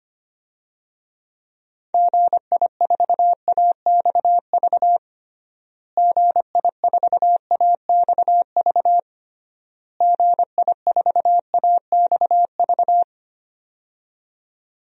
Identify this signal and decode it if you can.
Morse code, heard as a keyed tone, about 25 wpm, 700 Hz. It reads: GI4AXV GI4AXV GI4AXV
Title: DXCC-100 Call Signs - 6 Characters - Repeated 3x - 25wpm